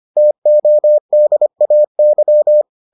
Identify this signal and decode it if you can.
Morse code, heard as a keyed tone, about 25 wpm, 600 Hz. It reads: TODAY